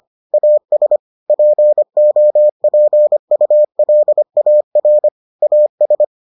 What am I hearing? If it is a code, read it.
Morse code, heard as a keyed tone, about 25 wpm, 600 Hz. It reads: AS POPULAR AS